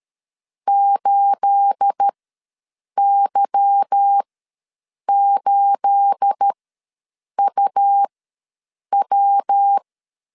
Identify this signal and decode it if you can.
Morse code, heard as a keyed tone, about 13 wpm, spaced as for 7 wpm, 800 Hz. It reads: 8Y8UW
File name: LEZIONI / 9 28 ws 12wpm 3354